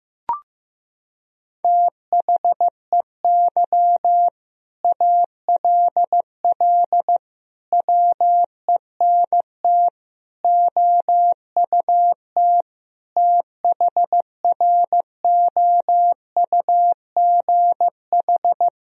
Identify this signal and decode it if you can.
Morse code, heard as a keyed tone, about 15 wpm, 700 Hz. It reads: THEY ALL WENT OUT THROUGH
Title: Sentences from Top 1000 Words - Review Part 1 - 15wpm